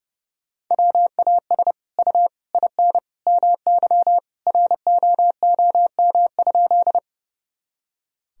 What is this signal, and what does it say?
Morse code, heard as a keyed tone, about 30 wpm, 700 Hz. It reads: WAS U IN MY ROOM?
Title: Sentences from Top 300 Words and Abbreviations 30wpm